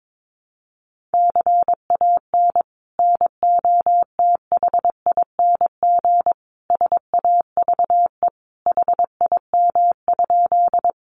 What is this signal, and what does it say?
Morse code, heard as a keyed tone, about 22 wpm, 700 Hz. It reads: CAN NOTHING SAVE HIM?
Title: Sentences from Top 1100 Words 22wpm